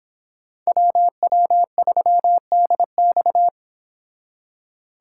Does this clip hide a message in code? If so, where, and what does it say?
Yes: Morse code from 0.7 to 3.5 s, heard as a keyed tone, about 25 wpm, 700 Hz. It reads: WW3DX